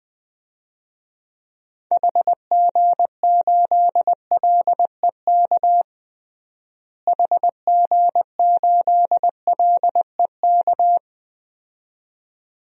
Morse code, heard as a keyed tone, about 21 wpm, 700 Hz. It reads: HG8LEK HG8LEK
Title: DXCC-100 Call Signs - 6 Characters - Repeated 2x - 20wpm